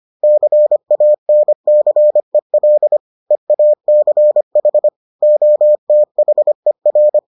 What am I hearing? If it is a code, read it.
Morse code, heard as a keyed tone, about 25 wpm, 600 Hz. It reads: CANCEL EACH OTHER